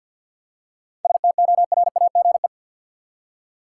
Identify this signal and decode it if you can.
Morse code, heard as a keyed tone, about 47 wpm, 700 Hz. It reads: STORAGE
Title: Seven Letter Words - 50wpm